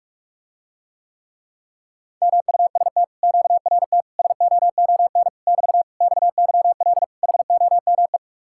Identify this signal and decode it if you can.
Morse code, heard as a keyed tone, about 44 wpm, 700 Hz. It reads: MUST QRT SOON <BT> XYL HOME